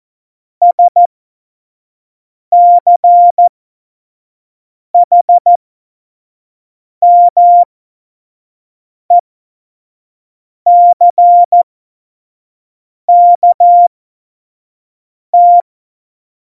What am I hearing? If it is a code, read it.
Morse code, heard as a keyed tone, about 14 wpm, spaced as for 5 wpm, 700 Hz. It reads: SCHMECKT
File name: Morsekurs / Witze / Witz 22. Oktober 2019, 25 F70